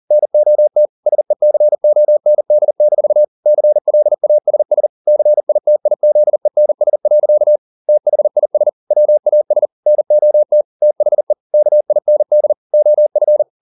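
Morse code, heard as a keyed tone, about 40 wpm, 600 Hz. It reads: NOT SECOND-CLASS CITIZENS. THIS WAS NOT THE KIND OF
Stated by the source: Morse code at 40 wpm, practice text, CW med Gnister.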